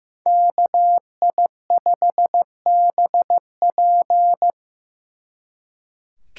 Morse code, heard as a keyed tone, about 18 wpm, 700 Hz. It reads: KI5BP
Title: Speed Racing - USA 2x2 Call Signs 15wpm